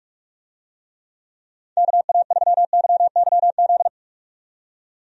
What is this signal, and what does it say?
Morse code, heard as a keyed tone, about 43 wpm, 700 Hz. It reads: KA3YYZ
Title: USA Call Signs - 6 Characters - 45wpm